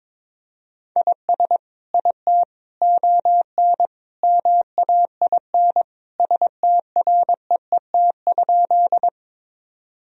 Morse code, heard as a keyed tone, about 22 wpm, 700 Hz. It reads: IS IT ON MAIN STREET?
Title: Sentences from Top 1000 Words - Review Part 2 - 22wpm